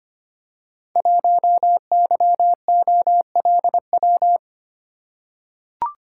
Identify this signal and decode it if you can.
Morse code, heard as a keyed tone, about 25 wpm, 700 Hz. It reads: 1YOLW
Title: Code Groups - 5 Letter:Numbers - 25wpm